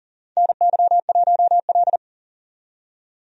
Morse code, heard as a keyed tone, about 40 wpm, 700 Hz. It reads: NY1L
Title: USA Call Signs - 4 Characters - 40wpm